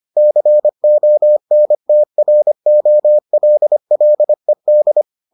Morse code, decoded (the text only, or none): CONTROLLED